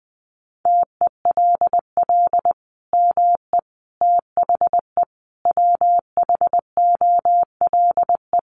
Morse code, heard as a keyed tone, about 20 wpm, 700 Hz. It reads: TELL ME THE WHOLE
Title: Sentences from Top 800 Words 20wpm